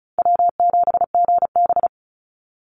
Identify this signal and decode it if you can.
Morse code, heard as a keyed tone, about 34 wpm, 700 Hz. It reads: W7GB
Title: Easy Call Signs Repeated 2x - 35wpm